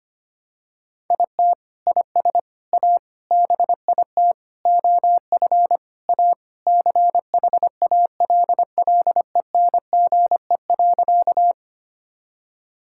Morse code, heard as a keyed tone, about 25 wpm, 700 Hz. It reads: IT IS A BIT OF A CHALLENGE.
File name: practice / 25wpm / Sentences with 4-Word Phrases - 25wpm